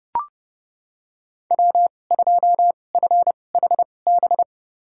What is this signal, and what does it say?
Morse code, heard as a keyed tone, about 30 wpm, spaced as for 22 wpm, 700 Hz. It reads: W2FHB
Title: USA Call Signs - 2x Character Spacing - 30wpm